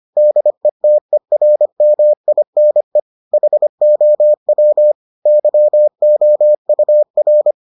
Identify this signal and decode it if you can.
Morse code, heard as a keyed tone, about 25 wpm, 600 Hz. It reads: DETERMINE HOW YOUR